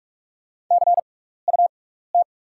Morse code, heard as a keyed tone, about 44 wpm, spaced as for 15 wpm, 690 Hz. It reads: CUT
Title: 6x Character Spacing - Top 300 Words - 45wpm